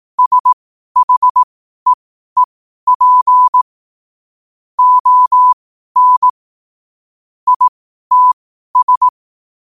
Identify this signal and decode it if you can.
Morse code, heard as a keyed tone, about 18 wpm, spaced as for 12 wpm, 1000 Hz. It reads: SHEEP ON ITS
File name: qotdfiles / 12wpm